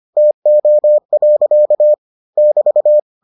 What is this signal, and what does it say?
Morse code, heard as a keyed tone, about 25 wpm, 600 Hz. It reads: TO. <BT>